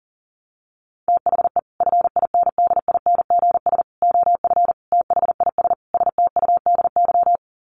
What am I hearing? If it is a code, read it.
Morse code, heard as a keyed tone, about 40 wpm, 700 Hz. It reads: THE FINDINGS OF THIS STUDY
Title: Sets of 5 Words Encore2 40wpm